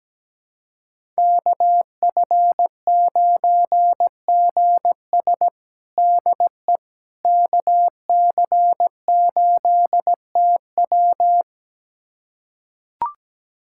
Morse code, heard as a keyed tone, about 17 wpm, 700 Hz. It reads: KF9GS DE KC8TW
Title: USA Callsign DE Callsign - 5 Characters - 17wpm